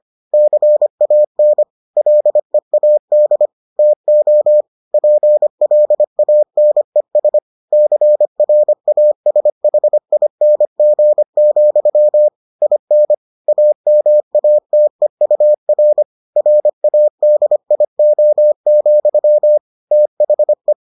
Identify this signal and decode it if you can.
Morse code, heard as a keyed tone, about 25 wpm, 600 Hz. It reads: CAN LEAD TO PLANES CRASHING, IN AMATEUR RADIO, THE